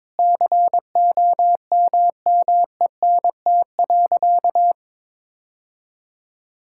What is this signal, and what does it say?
Morse code, heard as a keyed tone, about 22 wpm, 700 Hz. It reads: COMMENT.
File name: Sentences with 7-Word Phrases - 22wpm